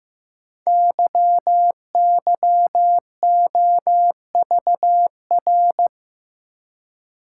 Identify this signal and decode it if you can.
Morse code, heard as a keyed tone, about 15 wpm, 700 Hz. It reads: YYOVR
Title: Code Groups - 5 Letters - 15wpm